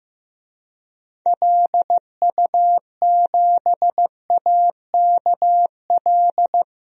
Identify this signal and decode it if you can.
Morse code, heard as a keyed tone, about 15 wpm, 700 Hz. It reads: LU7AKL
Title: DXCC-100 Call Signs - 6 Characters - Repeated 3x - 15wpm